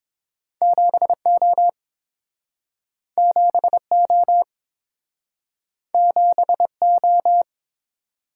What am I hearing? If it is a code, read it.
Morse code, heard as a keyed tone, about 30 wpm, 700 Hz. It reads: 7O 7O 7O